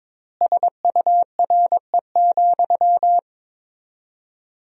Morse code, heard as a keyed tone, about 22 wpm, 700 Hz. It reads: SURE,
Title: 4x Word-Spacing - Sentences from Top 500 Words - 22wpm